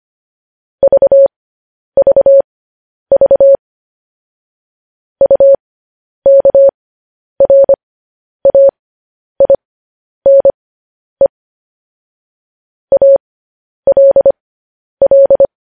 Morse code, heard as a keyed tone, about 25 wpm, spaced as for 10 wpm, 550 Hz. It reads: VVV UKRAINE ALL
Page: News Headlines in Morse Code at 10 WPM